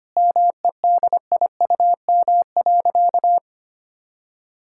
Morse code, heard as a keyed tone, about 25 wpm, 700 Hz. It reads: MEDIUM.
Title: Sentences from Top 1000 Words - Review Part 6 - 25wpm